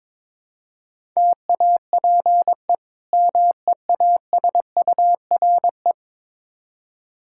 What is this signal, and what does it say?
Morse code, heard as a keyed tone, about 22 wpm, 700 Hz. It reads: TAPE MEASURE
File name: practice / 22wpm / Difficult Words with Hints - 22wpm